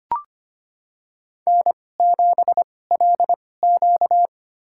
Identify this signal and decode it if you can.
Morse code, heard as a keyed tone, about 25 wpm, spaced as for 18 wpm, 700 Hz. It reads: N7LQ
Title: USA Call Signs - 2x Character Spacing - 25wpm